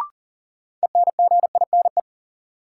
Morse code, heard as a keyed tone, about 40 wpm, 700 Hz. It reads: ENGINE
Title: Sentences from Top 1500 Words - New Word Drill - 40wpm